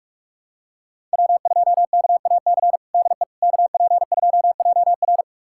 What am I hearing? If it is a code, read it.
Morse code, heard as a keyed tone, about 46 wpm, 700 Hz. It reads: W2KAC DE KP2JR
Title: USA Callsign DE Callsign - 5 Characters - 45wpm